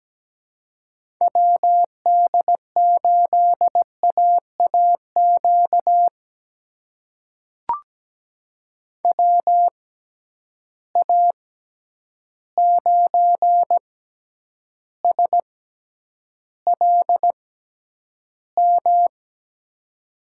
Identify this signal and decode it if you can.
Morse code, heard as a keyed tone, about 17 wpm, spaced as for 9 wpm, 700 Hz. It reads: WD8AAQ WA9SLM
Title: USA Call Signs - 6 Characters - 6x Character Spacing - 17wpm